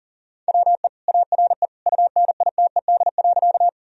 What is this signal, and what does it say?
Morse code, heard as a keyed tone, about 40 wpm, 700 Hz. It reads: WE ARE UNITED.
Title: Sentences from Top 1000 Words - Review Part 4 - 40wpm